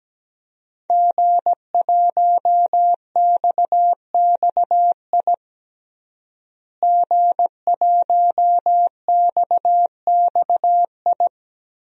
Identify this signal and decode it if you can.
Morse code, heard as a keyed tone, about 17 wpm, 700 Hz. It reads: G1XXI G1XXI
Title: DXCC-100 Call Signs - 5 Characters - Repeated 2x - 17wpm